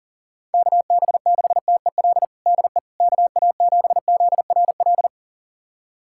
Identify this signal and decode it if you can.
Morse code, heard as a keyed tone, about 40 wpm, 700 Hz. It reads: KB6TEL DE KA7ZRL
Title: USA Callsign DE Callsign 40wpm